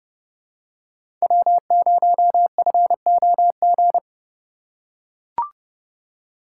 Morse code, heard as a keyed tone, about 30 wpm, 700 Hz. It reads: W0FOG
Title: USA Call Signs - 5 Characters - 4x Character Spacing - 30wpm